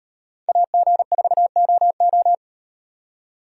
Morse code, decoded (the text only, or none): AG4OO